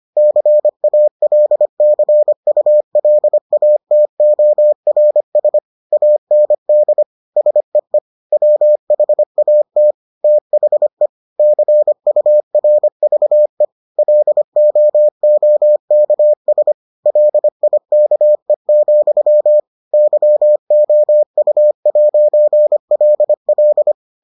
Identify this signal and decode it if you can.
Morse code, heard as a keyed tone, about 25 wpm, 600 Hz. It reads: CALCULATORS AND SEE WHAT THE CURVE LOOKS LIKE, YOU'LL